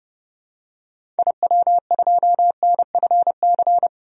Morse code, heard as a keyed tone, about 30 wpm, 700 Hz. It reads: IW2NFC
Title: DXCC-100 Call Signs - 6 Characters - 30wpm